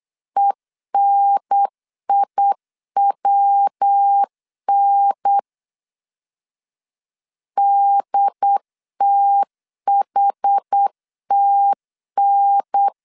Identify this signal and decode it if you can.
Morse code, heard as a keyed tone, about 8 wpm, 800 Hz. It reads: ENIWN DTHTN